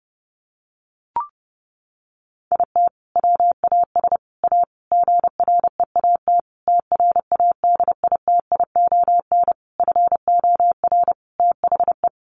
Morse code, heard as a keyed tone, about 30 wpm, 700 Hz. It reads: IT WAS A GREAT TRADITION FOR THE
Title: Sentences from Top 1500 Words 30wpm